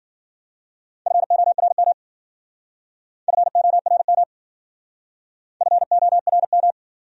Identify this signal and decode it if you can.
Morse code, heard as a keyed tone, about 50 wpm, 700 Hz. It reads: FORM FORM FORM